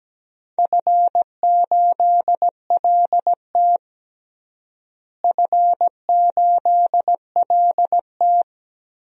Morse code, heard as a keyed tone, about 17 wpm, 700 Hz. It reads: F8LT F8LT
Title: DXCC-100 Call Signs - 4 Characters - Repeated 2x - 17wpm